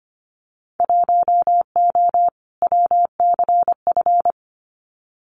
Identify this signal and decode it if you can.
Morse code, heard as a keyed tone, about 25 wpm, 700 Hz. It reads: 1O WCF
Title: Winter Field Day Exchanges 25wpm